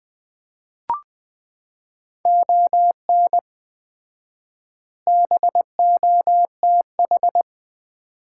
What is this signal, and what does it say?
Morse code, heard as a keyed tone, about 19 wpm, 700 Hz. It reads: ON BOTH
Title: 4x Word-Spacing - Sets of 5 Words - 20wpm